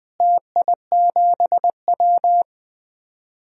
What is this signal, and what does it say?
Morse code, heard as a keyed tone, about 20 wpm, 700 Hz. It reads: TI7W
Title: DXCC-100 Call Signs - 4 Characters - Repeated 3x - 20wpm